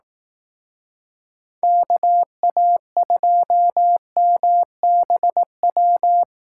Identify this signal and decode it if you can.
Morse code, heard as a keyed tone, about 19 wpm, 700 Hz. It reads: KA2MBW